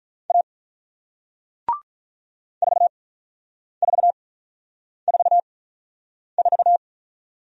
Morse code, heard as a keyed tone, about 47 wpm, spaced as for 9 wpm, 700 Hz. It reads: A 4444